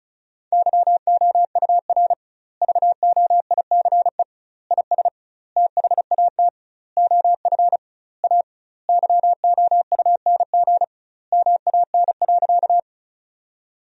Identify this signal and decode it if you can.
Morse code, heard as a keyed tone, about 36 wpm, 700 Hz. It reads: YOUR VOICE IS THAT OF A YOUNG MAN.
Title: 2x Word-Spacing - Sentences from Top 500 Words - 35wpm